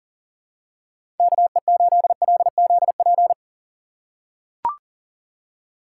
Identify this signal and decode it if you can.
Morse code, heard as a keyed tone, about 40 wpm, 700 Hz. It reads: KE8LZP
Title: USA Call Signs - 6 Characters - 3x Character Spacing - 40wpm